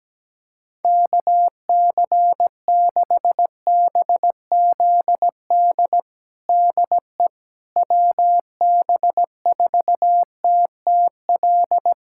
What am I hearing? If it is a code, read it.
Morse code, heard as a keyed tone, about 17 wpm, 700 Hz. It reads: KC6BZD DE WB4TTL
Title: USA Callsign DE Callsign - 6 Characters - 17wpm